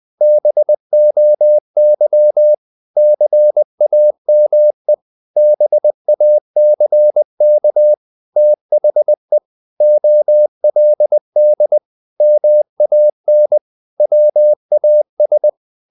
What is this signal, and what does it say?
Morse code, heard as a keyed tone, about 20 wpm, 600 Hz. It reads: BOY CAME BACK THE OLD MAN WAS
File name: the old man and the sea / the old man and the sea part 1 20wpm